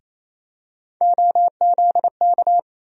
Morse code, heard as a keyed tone, about 28 wpm, 700 Hz. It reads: OZK
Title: USA Call Sign Postfixes - 3 Letters 28wpm